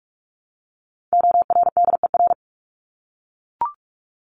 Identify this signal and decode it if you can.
Morse code, heard as a keyed tone, about 44 wpm, 700 Hz. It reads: ORDER
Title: Top 500 Words 45wpm